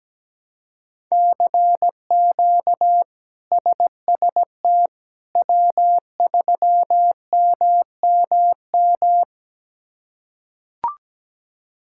Morse code, heard as a keyed tone, about 17 wpm, 700 Hz. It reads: CQ SST W3MMM
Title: K1USN SST QSO Practice - 17wpm